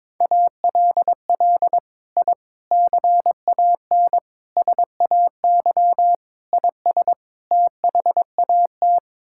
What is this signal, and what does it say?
Morse code, heard as a keyed tone, about 22 wpm, 700 Hz. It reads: ALL I CAN SAY IS THAT
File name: practice / 22wpm / Sets of 6 Words - 22wpm